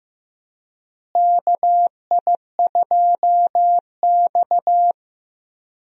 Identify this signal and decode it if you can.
Morse code, heard as a keyed tone, about 15 wpm, 700 Hz. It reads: KI2X
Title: USA Call Signs - 4 Characters - Repeated 3x - 15wpm